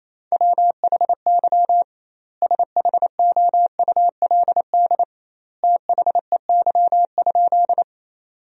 Morse code, heard as a keyed tone, about 28 wpm, 700 Hz. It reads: WHY SHOULD THEY?
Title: 2x Word-Spacing - Sentences from Top 300 Words - 28wpm